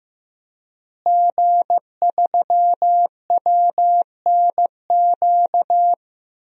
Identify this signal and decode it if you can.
Morse code, heard as a keyed tone, about 16 wpm, 700 Hz. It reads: G3WNQ
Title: DXCC-100 Call Signs - 5 Characters - Repeated 3x - 15wpm